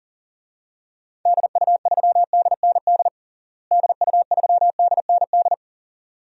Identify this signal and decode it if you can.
Morse code, heard as a keyed tone, about 37 wpm, 700 Hz. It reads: DU3DND DU3DND
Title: DXCC-100 Call Signs - 6 Characters - Repeated 2x - 40wpm